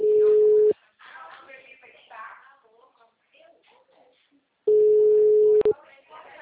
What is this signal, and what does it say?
Ringback tone (Europe, 425 Hz), with speech.